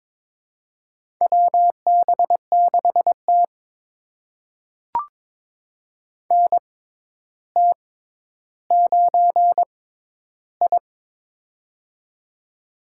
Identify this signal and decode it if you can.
Morse code, heard as a keyed tone, about 22 wpm, spaced as for 11 wpm, 700 Hz. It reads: WB6T NT9I
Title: USA Call Signs - 6x Character Spacing - 22wpm